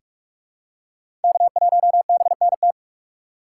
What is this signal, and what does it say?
Morse code, heard as a keyed tone, about 46 wpm, 700 Hz. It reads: K1BNT
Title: USA Call Signs - 5 Characters - 2x Character Spacing - 45wpm